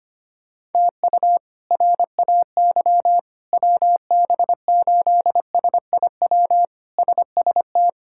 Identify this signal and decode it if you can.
Morse code, heard as a keyed tone, about 25 wpm, 700 Hz. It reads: TU RAY WB8SIW SST
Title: K1USN SST QSO Practice - 25wpm